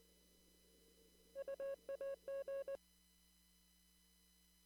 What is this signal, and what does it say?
Morse code, heard as a keyed tone, about 20 wpm, 550 Hz. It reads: UAG
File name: CW / kt5j wpx cw-op-k5pi